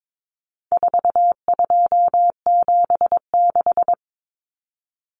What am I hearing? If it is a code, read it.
Morse code, heard as a keyed tone, about 22 wpm, 700 Hz. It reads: 4276
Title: Common Cut Numbers - 4-Digit 22wpm